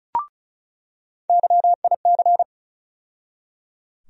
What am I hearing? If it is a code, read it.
Morse code, heard as a keyed tone, about 34 wpm, 700 Hz. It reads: YIC
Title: USA Call Sign Postfixes - 3 Letters 35wpm